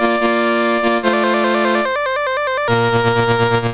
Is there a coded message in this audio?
No.